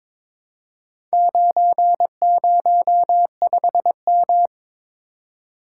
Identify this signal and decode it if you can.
Morse code, heard as a keyed tone, about 22 wpm, 700 Hz. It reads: 905M
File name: practice / 22wpm / Four Letter-Number Combinations - 22wpm